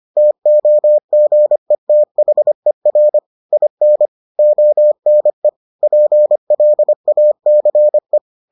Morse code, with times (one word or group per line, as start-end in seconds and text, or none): 0.2-3.2 TOGETHER
3.5-4.1 IN
4.4-5.5 ONE
5.8-8.2 PLACE